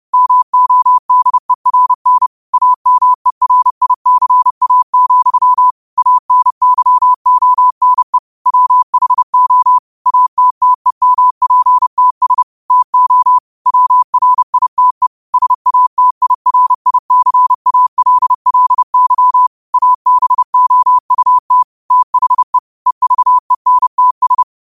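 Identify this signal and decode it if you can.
Morse code, heard as a keyed tone, about 30 wpm, 1000 Hz. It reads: MODERN AMERICA, ANYONE WHO ATTEMPTS TO WRITE SATIRICALLY ABOUT THE EVENTS
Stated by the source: Morse quote of the day at 30 WPM.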